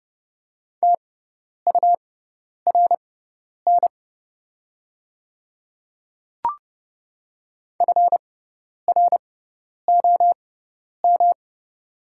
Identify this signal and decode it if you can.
Morse code, heard as a keyed tone, about 30 wpm, spaced as for 10 wpm, 700 Hz. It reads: TURN FROM
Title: 6x Character Spacing - Top 300 Words - 30wpm